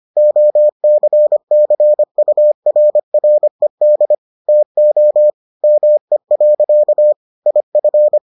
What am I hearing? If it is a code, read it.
Morse code, heard as a keyed tone, about 25 wpm, 600 Hz. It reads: OCCURRED TO ME. IF